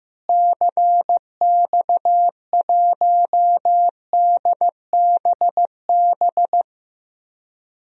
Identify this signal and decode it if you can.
Morse code, heard as a keyed tone, about 15 wpm, 700 Hz. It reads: CX1DBB
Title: DXCC-100 Call Signs - 6 Characters - Repeated 2x - 15wpm